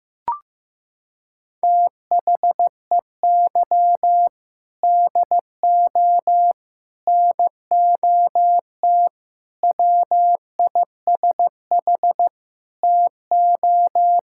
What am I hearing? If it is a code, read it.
Morse code, heard as a keyed tone, about 15 wpm, 700 Hz. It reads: THEY DO NOT WISH TO
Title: Sentences from Top 1600 Words 15wpm